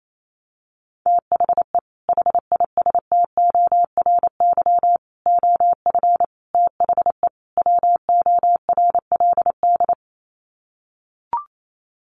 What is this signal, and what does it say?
Morse code, heard as a keyed tone, about 28 wpm, 700 Hz. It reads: THE HISTORY OF THE WORLD
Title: Sets of 5 Words Encore2 28wpm